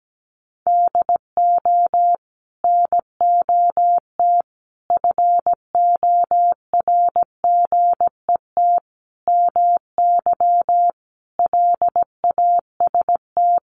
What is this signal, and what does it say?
Morse code, heard as a keyed tone, about 17 wpm, 700 Hz. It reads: DO NOT FORGET MY LAST
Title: Sentences from Top 1200 Words 17wpm